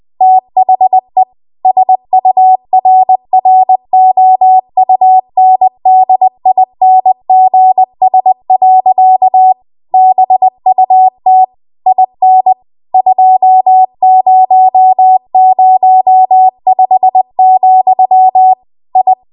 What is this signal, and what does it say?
Morse code, heard as a keyed tone, about 20 wpm, 750 Hz. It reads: THE SURROUNDINGS. BUT IN 2005, I
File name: Morse / 20WPM